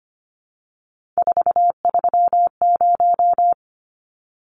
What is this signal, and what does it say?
Morse code, heard as a keyed tone, about 25 wpm, 700 Hz. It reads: 430
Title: Numbers - 3-Digit 25wpm